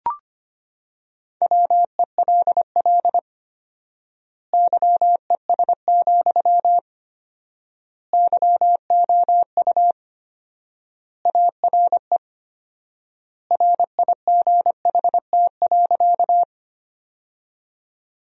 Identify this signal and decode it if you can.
Morse code, heard as a keyed tone, about 25 wpm, 700 Hz. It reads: WELL YES, YOU ARE RIGHT.
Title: 4x Word-Spacing - Sentences from Top 500 Words - 25wpm